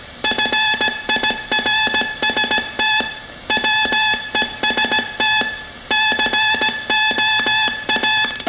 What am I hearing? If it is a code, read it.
Morse code, heard as a keyed tone, about 17 wpm, 860 Hz. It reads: FIRST WEST COA